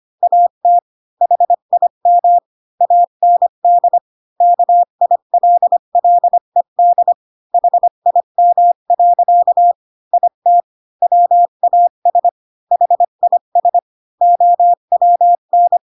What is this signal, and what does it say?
Morse code, heard as a keyed tone, about 25 wpm, 700 Hz. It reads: AT HIM AND KILLED HIM. IT WAS HIS OWN
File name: Pinocchio / Pinocchio ch7 20 25 ik1hge